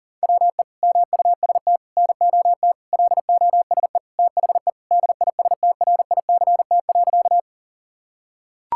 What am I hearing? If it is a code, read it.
Morse code, heard as a keyed tone, about 40 wpm, 700 Hz. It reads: WE MUST NOT LOSE THE DISTRICT.
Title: Sentences from Top 1600 Words 40wpm